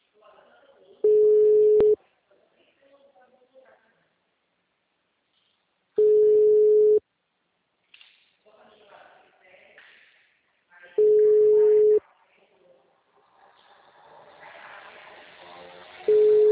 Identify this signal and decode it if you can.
Ringback tone (Europe, 425 Hz), with speech.
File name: out- -2006- - - .19392